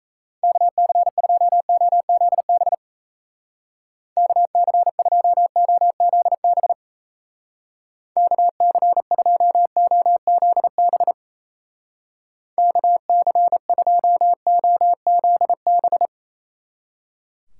Morse code, heard as a keyed tone, about 35 wpm, 700 Hz. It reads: KC2OZB KC2OZB KC2OZB KC2OZB
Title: Speed Racing - USA 2x3 Call Signs 28wpm